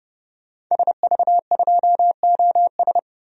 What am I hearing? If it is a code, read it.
Morse code, heard as a keyed tone, about 30 wpm, 700 Hz. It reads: SV2OS
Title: DXCC-100 Call Signs - 5 Characters - 30wpm